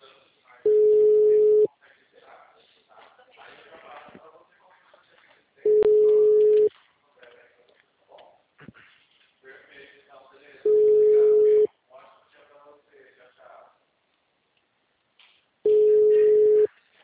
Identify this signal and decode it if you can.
Ringback tone (Europe, 425 Hz), with speech.